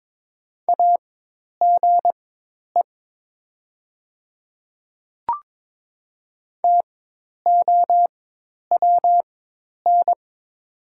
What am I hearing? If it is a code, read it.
Morse code, heard as a keyed tone, about 22 wpm, spaced as for 10 wpm, 700 Hz. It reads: AGE TOWN
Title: 4x Character Spacing - Top 500 Words - 22wpm